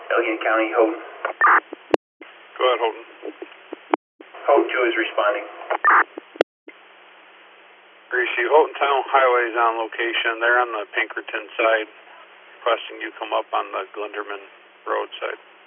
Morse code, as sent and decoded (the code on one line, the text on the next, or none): none